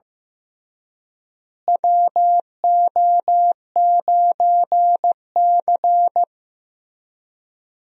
Morse code, heard as a keyed tone, about 15 wpm, 700 Hz. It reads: WO9C
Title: USA Call Signs - 4 Characters - Repeated 3x - 15wpm